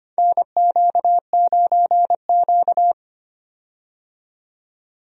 Morse code, heard as a keyed tone, about 25 wpm, 700 Hz. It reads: NQ9Q
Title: USA Call Signs - 4 Characters - 25wpm